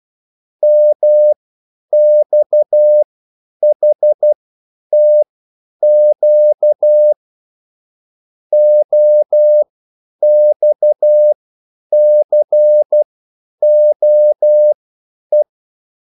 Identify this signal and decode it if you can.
Morse code, heard as a keyed tone, about 12 wpm, spaced as for 9 wpm, 600 Hz. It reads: MXHTQ OXCOE
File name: QSL / CW lesson